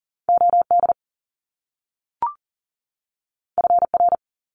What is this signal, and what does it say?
Morse code, heard as a keyed tone, about 40 wpm, 700 Hz. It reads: OD FR
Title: Top Two Letter Combinations in Words 40wpm